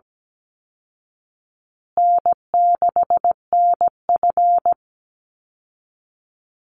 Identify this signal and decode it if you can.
Morse code, heard as a keyed tone, about 17 wpm, 700 Hz. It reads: N6NF
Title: Easy Call Signs Repeated 3x - 17wpm